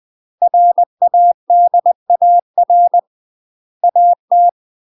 Morse code, heard as a keyed tone, about 20 wpm, 700 Hz. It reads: RADAR AT
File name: UK MASTER QSOs TRAINING / 20WPM / UK MASTER QSOs 0035